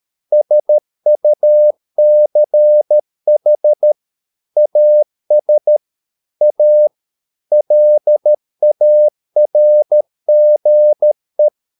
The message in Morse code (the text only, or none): SUCH AS A LARGE